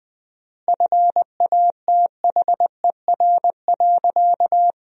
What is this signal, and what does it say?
Morse code, heard as a keyed tone, about 20 wpm, 700 Hz. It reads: FATHER.
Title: 4x Word-Spacing - Sentences from Top 300 Words - 20wpm